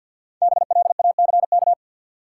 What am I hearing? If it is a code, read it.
Morse code, heard as a keyed tone, about 53 wpm, 700 Hz. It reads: BLACK
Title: Sentences from Top 400 Words - New Word Drill - 50wpm